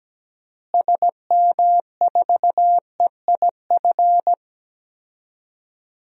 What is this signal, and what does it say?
Morse code, heard as a keyed tone, about 17 wpm, 700 Hz. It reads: SM4EIF
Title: DXCC-100 Call Signs - 6 Characters - Repeated 3x - 17wpm